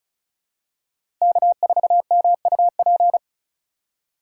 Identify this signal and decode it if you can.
Morse code, heard as a keyed tone, about 36 wpm, 700 Hz. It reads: K4MUP
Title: USA Call Signs - 5 Characters - 3x Character Spacing - 35wpm